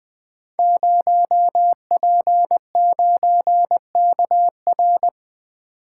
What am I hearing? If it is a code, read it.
Morse code, heard as a keyed tone, about 20 wpm, 700 Hz. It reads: 0P9KR
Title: Code Groups - 5 Letter:Numbers - 20wpm